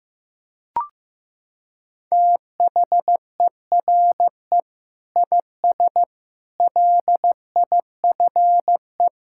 Morse code, heard as a keyed tone, about 15 wpm, 700 Hz. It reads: THERE IS LIFE
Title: Sentences from Top 1000 Words - Review Part 6 - 15wpm